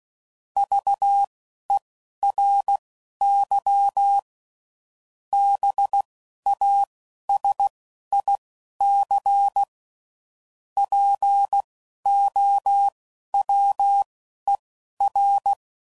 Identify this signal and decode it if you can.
Morse code, heard as a keyed tone, about 16 wpm, spaced as for 11 wpm, 800 Hz. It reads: VERY BASIC POWER
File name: CW Code Groups @10WPM / Technical Subjects / SwPwrSup